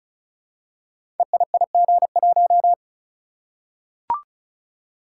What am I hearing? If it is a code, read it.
Morse code, heard as a keyed tone, about 34 wpm, 700 Hz. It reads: EIIG1
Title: Code Groups - 5 Letter:Numbers - 35wpm